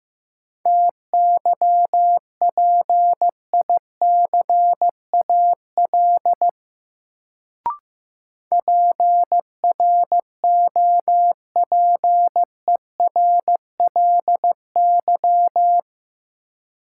Morse code, heard as a keyed tone, about 15 wpm, 700 Hz. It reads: TYPICAL PROPERLY
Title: Sentences from Top 1100 Words - New Word Drill - 15wpm